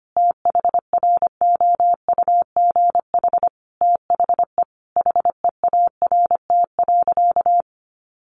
Morse code, heard as a keyed tone, about 25 wpm, 700 Hz. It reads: THROUGH THE HEART.
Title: Sentences from Top 1200 Words 25wpm